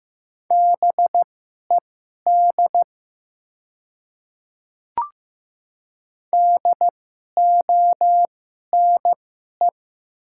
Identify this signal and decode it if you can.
Morse code, heard as a keyed tone, about 15 wpm, spaced as for 11 wpm, 690 Hz. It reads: BED DONE
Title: 2x Character Spacing - Top 500 Words - 15wpm